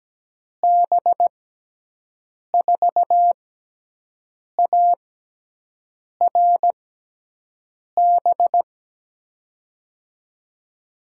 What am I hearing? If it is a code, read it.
Morse code, heard as a keyed tone, about 17 wpm, spaced as for 6 wpm, 700 Hz. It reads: B4ARB
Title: USA Call Signs - 6 Characters - 6x Character Spacing - 17wpm